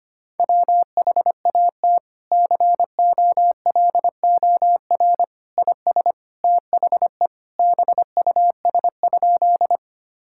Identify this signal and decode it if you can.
Morse code, heard as a keyed tone, about 25 wpm, 700 Hz. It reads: WHAT COLOR IS THE BUS?
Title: Sentences from Top 1000 Words - Review Part 6 - 25wpm